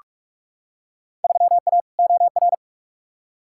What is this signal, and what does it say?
Morse code, heard as a keyed tone, about 46 wpm, 700 Hz. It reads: 3A OR
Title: Field Day Exchanges - 45wpm